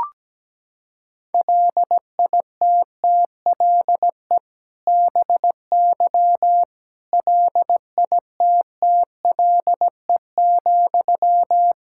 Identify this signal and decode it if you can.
Morse code, heard as a keyed tone, about 17 wpm, 700 Hz. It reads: LITTLE BY LITTLE,
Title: Sentences from Top 1600 Words 17wpm